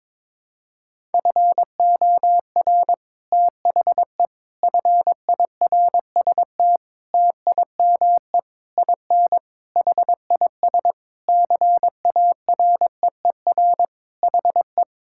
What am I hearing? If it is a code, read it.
Morse code, heard as a keyed tone, about 22 wpm, 700 Hz. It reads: FOR THE FIRST TIME IN HIS CAREER HE
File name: Sentences with 7-Word Phrases - 22wpm